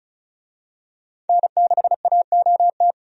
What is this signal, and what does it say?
Morse code, heard as a keyed tone, about 36 wpm, 700 Hz. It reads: N6AOT
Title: USA Call Signs - 5 Characters - 35wpm